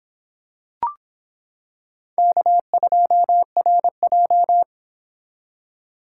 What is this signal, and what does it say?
Morse code, heard as a keyed tone, about 26 wpm, 700 Hz. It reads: K2RJ